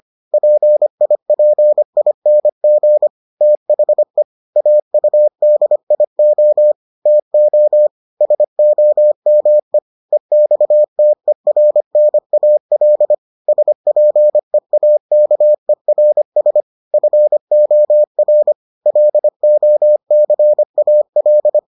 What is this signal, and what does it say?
Morse code, heard as a keyed tone, about 25 wpm, 600 Hz. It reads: PIPING THE AUDIO TO SOME EXTERNAL SPEAKERS FOR LOCAL